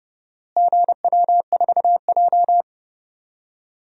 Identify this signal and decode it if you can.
Morse code, heard as a keyed tone, about 30 wpm, 700 Hz. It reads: GW4J
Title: DXCC-100 Call Signs - 4 Characters - Repeated 3x - 30wpm